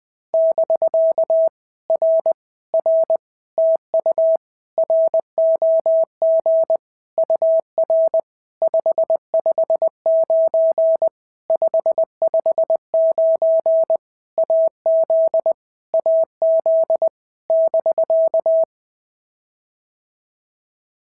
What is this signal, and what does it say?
Morse code, heard as a keyed tone, about 20 wpm, 650 Hz. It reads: <BK> R R TU ROG UR 559 559 AZ AZ <BK>
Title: POTA QSO Practice - 20wpm